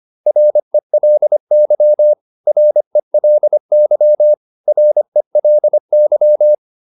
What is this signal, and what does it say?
Morse code, heard as a keyed tone, about 25 wpm, 600 Hz. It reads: RELY RELY RELY